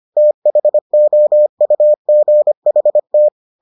Morse code, heard as a keyed tone, about 25 wpm, 600 Hz. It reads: THOUGHT